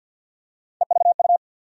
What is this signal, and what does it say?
Morse code, heard as a keyed tone, about 46 wpm, 700 Hz. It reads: EVU
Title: USA Call Sign Postfixes - 3 Letters 50wpm